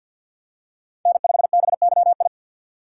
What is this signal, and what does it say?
Morse code, heard as a keyed tone, about 54 wpm, 700 Hz. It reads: N5BYI
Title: USA Call Signs - 5 Characters - 8x Character Spacing - 50wpm